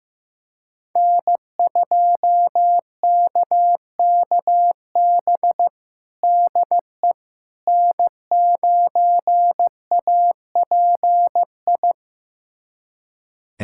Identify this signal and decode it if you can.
Morse code, heard as a keyed tone, about 15 wpm, 700 Hz. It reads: N2KKB DE N9API
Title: USA Callsign DE Callsign - 5 Characters - 15wpm